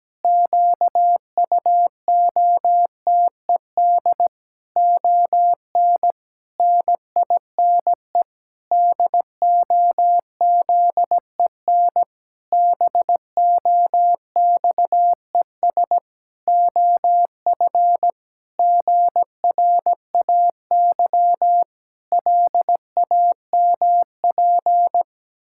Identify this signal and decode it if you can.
Morse code, heard as a keyed tone, about 17 wpm, 700 Hz. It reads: QUOTED ON NINE DOZEN BOXES OF GRAY LAMP
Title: Pangram Challenge v2 - 17wpm